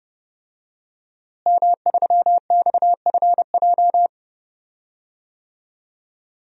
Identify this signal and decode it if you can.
Morse code, heard as a keyed tone, about 30 wpm, 700 Hz. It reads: M3XFJ
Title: DXCC-100 Call Signs - 5 Characters - Repeated 2x - 30wpm